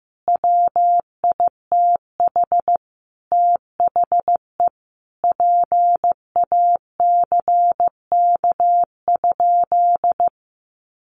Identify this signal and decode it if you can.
Morse code, heard as a keyed tone, about 14 wpm, 700 Hz. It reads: WITH THE PACK?
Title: Sentences from Top 1300 Words 15wpm